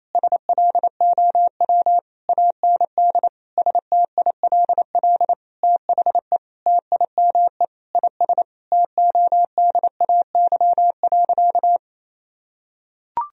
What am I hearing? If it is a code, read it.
Morse code, heard as a keyed tone, about 28 wpm, 700 Hz. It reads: SLOW AND STILL THE TIME IS TODAY.
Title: Sentences from Top 1000 Words - Review Part 2 - 28wpm